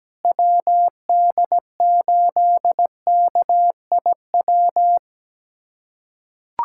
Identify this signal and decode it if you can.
Morse code, heard as a keyed tone, about 17 wpm, 700 Hz. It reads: WD8KIW
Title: USA Callsign DE Callsign - 6 Characters - 17wpm